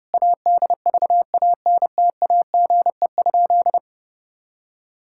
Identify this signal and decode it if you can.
Morse code, heard as a keyed tone, about 32 wpm, 700 Hz. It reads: ADVANTAGE?
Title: Sentences from Top 1000 Words - Review Part 6 - 30wpm